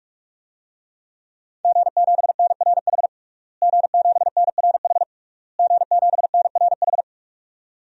Morse code, heard as a keyed tone, about 46 wpm, 700 Hz. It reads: G7NRH G7NRH G7NRH
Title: DXCC-100 Call Signs - 5 Characters - Repeated 3x - 45wpm